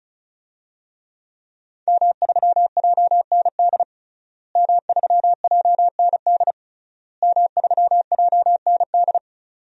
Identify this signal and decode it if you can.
Morse code, heard as a keyed tone, about 36 wpm, 700 Hz. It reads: M3JND M3JND M3JND
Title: DXCC-100 Call Signs - 5 Characters - Repeated 3x - 35wpm